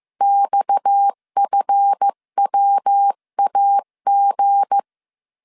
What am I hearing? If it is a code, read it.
Morse code, heard as a keyed tone, about 15 wpm, 800 Hz. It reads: XFWAG
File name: CW / DET / 1 9 1500C S B 14wpm 3302